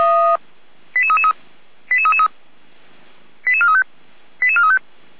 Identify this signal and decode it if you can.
Five-tone selective call (ZVEI), heard as quick runs of five tones, tones of about 70 ms. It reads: 88333 88333 88435 88435